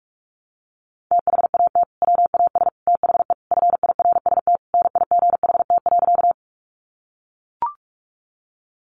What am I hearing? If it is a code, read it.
Morse code, heard as a keyed tone, about 44 wpm, 700 Hz. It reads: THAT WAS THE FIRST NIGHT.
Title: Sentences from Top 300 Words 45wpm